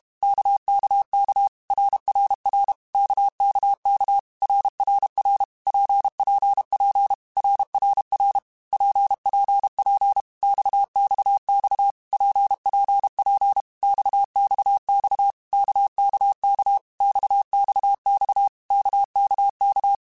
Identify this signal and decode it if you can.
Morse code, heard as a keyed tone, about 32 wpm, 800 Hz. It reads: KKK RRR KKK RRR PPP RRR PPP XXX PPP XXX KKK XXX KKK
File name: Basic Letter Groups By7Groups / RKPX / RKPX@30